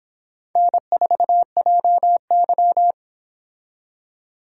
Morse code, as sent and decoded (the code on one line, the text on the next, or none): -. ....- .--- -.--
N4JY